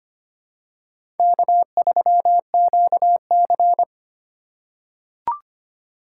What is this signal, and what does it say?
Morse code, heard as a keyed tone, about 25 wpm, 700 Hz. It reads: K3QC
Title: USA Call Signs - 4 Characters - 2x Character Spacing - 25wpm